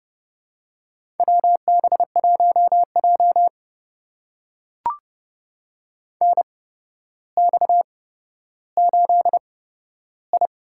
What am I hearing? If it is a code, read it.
Morse code, heard as a keyed tone, about 30 wpm, spaced as for 13 wpm, 700 Hz. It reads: WB1J NX8I
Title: USA Call Signs - 4 Characters - 8x Character Spacing - 30wpm